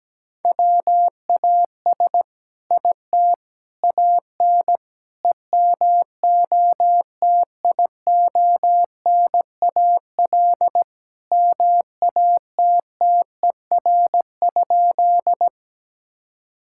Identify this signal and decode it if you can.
Morse code, heard as a keyed tone, about 17 wpm, 700 Hz. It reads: WAS IT AN EMOTIONAL MATTER?
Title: Sentences from Top 1000 Words - Review Part 5 - 17wpm